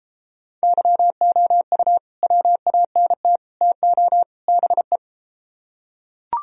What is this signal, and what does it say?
Morse code, heard as a keyed tone, about 33 wpm, 700 Hz. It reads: YOU WANT TO BE